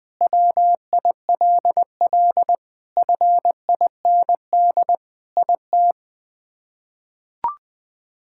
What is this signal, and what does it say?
Morse code, heard as a keyed tone, about 20 wpm, 700 Hz. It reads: WILL FIND IT
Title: Sentences with 3-Word Phrases - 20wpm